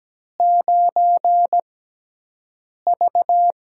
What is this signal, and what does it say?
Morse code, heard as a keyed tone, about 17 wpm, spaced as for 6 wpm, 700 Hz. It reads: 9V
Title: USA Call Signs - 4 Characters - 6x Character Spacing - 17wpm